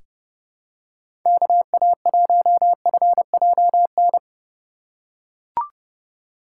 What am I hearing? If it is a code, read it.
Morse code, heard as a keyed tone, about 30 wpm, 700 Hz. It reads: KA1FJN
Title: USA Call Signs - 6 Characters - 2x Character Spacing - 30wpm